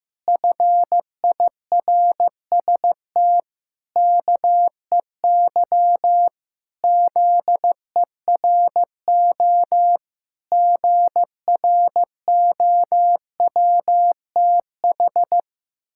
Morse code, heard as a keyed tone, about 15 wpm, 700 Hz. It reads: FIRST KEY ZERO GROWTH
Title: Pangram Challenge v2 - 15wpm